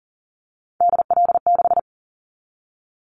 Morse code, heard as a keyed tone, about 40 wpm, 700 Hz. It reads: DL6
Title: Callsign Prefixes for Top 25 DXCC - 40wpm